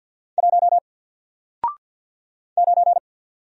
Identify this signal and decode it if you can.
Morse code, heard as a keyed tone, about 50 wpm, 700 Hz. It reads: 1 9
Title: Single Character (Letter:Number:Punctuation:Prosign) 50wpm